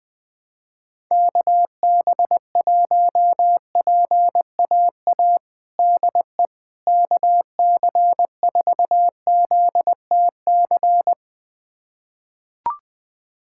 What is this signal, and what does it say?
Morse code, heard as a keyed tone, about 20 wpm, 700 Hz. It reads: KB1PAA DE KC4ZTC
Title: USA Callsign DE Callsign - 6 Characters - 20wpm